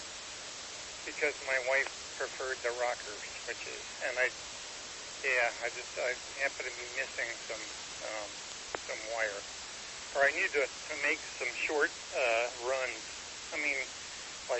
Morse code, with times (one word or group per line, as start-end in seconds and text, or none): none